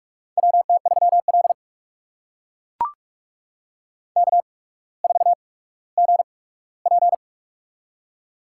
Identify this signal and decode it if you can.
Morse code, heard as a keyed tone, about 44 wpm, spaced as for 19 wpm, 700 Hz. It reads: WT3L K4GP
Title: USA Call Signs - 8x Character Spacing - 45wpm